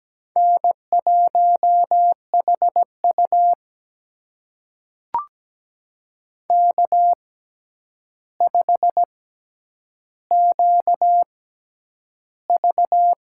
Morse code, heard as a keyed tone, about 17 wpm, spaced as for 9 wpm, 700 Hz. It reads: N1HU K5QV
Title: USA Call Signs - 4 Characters - 6x Character Spacing - 17wpm